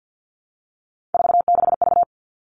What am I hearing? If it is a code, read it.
Morse code, heard as a keyed tone, about 50 wpm, 700 Hz. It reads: 46V
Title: All Cut Numbers - 3-Digit 50wpm